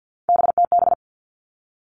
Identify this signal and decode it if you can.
Morse code, heard as a keyed tone, about 50 wpm, 700 Hz. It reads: BTB
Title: All Cut Numbers - 3-Digit 50wpm